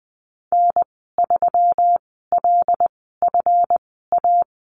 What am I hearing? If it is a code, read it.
Morse code, heard as a keyed tone, about 19 wpm, spaced as for 14 wpm, 700 Hz. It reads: N3LFA
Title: 161 Lesson 26g - Letter g - Call Signs 20wpm